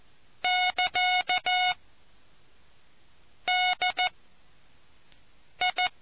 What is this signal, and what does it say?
Morse code, heard as a keyed tone, about 14 wpm, 730 Hz. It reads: <KA> D I